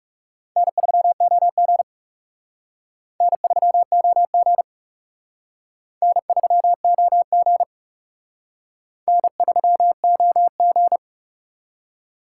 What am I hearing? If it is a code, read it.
Morse code, heard as a keyed tone, about 38 wpm, 700 Hz. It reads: N3OG N3OG N3OG N3OG